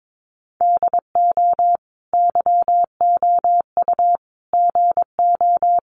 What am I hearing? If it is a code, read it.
Morse code, heard as a keyed tone, about 22 wpm, 700 Hz. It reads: DO YOU GO
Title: Sentences from Top 700 Words 22wpm